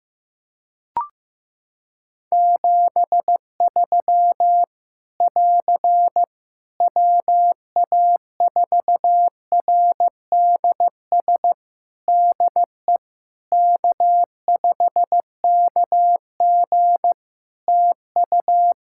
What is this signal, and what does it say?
Morse code, heard as a keyed tone, about 15 wpm, 700 Hz. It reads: 73 <AR> WA4RDS DE K5KG TU
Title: Practice QSO - Exchange 4 - Ending - 15wpm